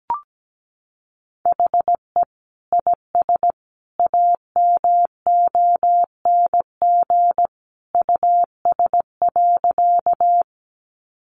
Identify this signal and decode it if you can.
Morse code, heard as a keyed tone, about 17 wpm, 700 Hz. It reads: HE IS AMONG US.
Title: Sentences from Top 500 Words 17wpm